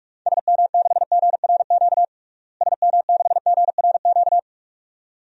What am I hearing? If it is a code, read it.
Morse code, heard as a keyed tone, about 46 wpm, 700 Hz. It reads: SM6GRQ SM6GRQ
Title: DXCC-100 Call Signs - 6 Characters - Repeated 2x - 45wpm